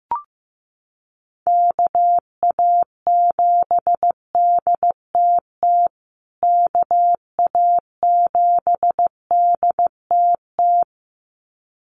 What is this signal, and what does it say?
Morse code, heard as a keyed tone, about 15 wpm, 700 Hz. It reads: KA7DTT KA7DTT
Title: USA Call Signs Repeated 2x 15wpm